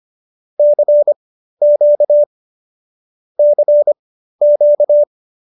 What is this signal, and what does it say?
Morse code, heard as a keyed tone, about 25 wpm, spaced as for 13 wpm, 570 Hz. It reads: CQ CQ
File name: qso208b 13